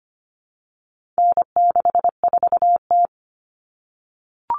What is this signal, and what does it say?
Morse code, heard as a keyed tone, about 25 wpm, 700 Hz. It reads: N64T
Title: Common Cut Numbers - 4-Digit 25wpm